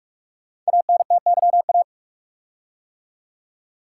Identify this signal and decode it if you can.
Morse code, heard as a keyed tone, about 44 wpm, 700 Hz. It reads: ANTYA
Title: Code Groups - 5 Letter:Numbers - 45wpm